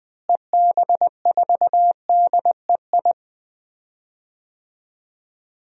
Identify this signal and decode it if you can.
Morse code, heard as a keyed tone, about 21 wpm, 700 Hz. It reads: EB4DEI
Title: DXCC-100 Call Signs - 6 Characters - Repeated 3x - 20wpm